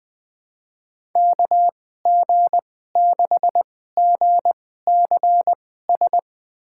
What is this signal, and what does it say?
Morse code, heard as a keyed tone, about 21 wpm, spaced as for 15 wpm, 700 Hz. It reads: KG6GCS
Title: USA Call Signs - 6 Characters - 2x Character Spacing - 20wpm